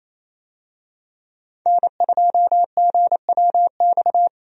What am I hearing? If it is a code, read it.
Morse code, heard as a keyed tone, about 28 wpm, 700 Hz. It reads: N2GWX
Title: USA Call Signs - 5 Characters - Repeated 2x - 28wpm